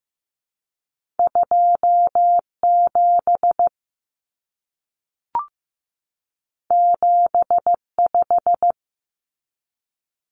Morse code, heard as a keyed tone, about 15 wpm, 700 Hz. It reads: 27 75
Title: Common Cut Numbers - 2-Digit 15wpm